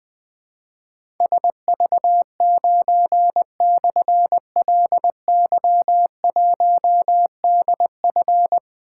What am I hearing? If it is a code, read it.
Morse code, heard as a keyed tone, about 20 wpm, 700 Hz. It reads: SV9/LY1DF
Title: Difficult Call Signs 20wpm